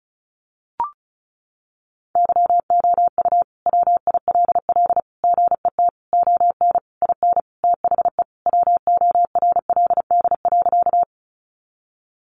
Morse code, heard as a keyed tone, about 34 wpm, 700 Hz. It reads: YOU WILL GET ON IN THE WORLD.
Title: Sentences from Top 200 Words 35wpm